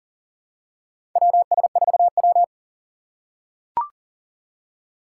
Morse code, heard as a keyed tone, about 40 wpm, 700 Hz. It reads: WS4W
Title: USA Call Signs - 4 Characters - 3x Character Spacing - 40wpm